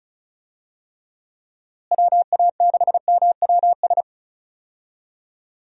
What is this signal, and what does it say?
Morse code, heard as a keyed tone, about 36 wpm, 700 Hz. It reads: WA6MWS
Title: USA Call Signs - 6 Characters - Repeated 3x - 35wpm